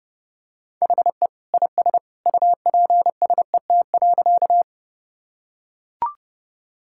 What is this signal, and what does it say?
Morse code, heard as a keyed tone, about 32 wpm, 700 Hz. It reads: HE IS UPSET.
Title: Sentences from Top 1000 Words - Review Part 5 - 30wpm